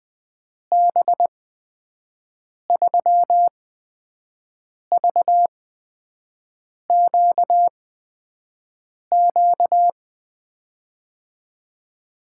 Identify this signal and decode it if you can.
Morse code, heard as a keyed tone, about 20 wpm, spaced as for 5 wpm, 700 Hz. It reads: B3VQQ
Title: USA Call Signs - 6 Characters - 8x Character Spacing - 20wpm